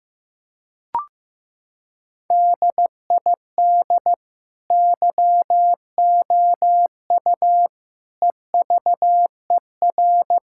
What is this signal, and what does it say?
Morse code, heard as a keyed tone, about 15 wpm, 700 Hz. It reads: DID YOU EVER
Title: Sentences from Top 1000 Words - Review Part 6 - 15wpm